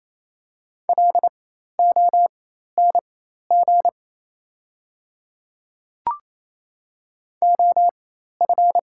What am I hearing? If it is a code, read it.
Morse code, heard as a keyed tone, about 27 wpm, spaced as for 13 wpm, 690 Hz. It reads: LONG OF
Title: 4x Character Spacing - Top 100 Words - 28wpm